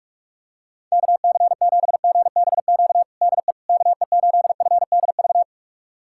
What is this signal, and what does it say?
Morse code, heard as a keyed tone, about 46 wpm, 700 Hz. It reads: KC7GBQ DE KE8FDV
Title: USA Callsign DE Callsign - 6 Characters - 45wpm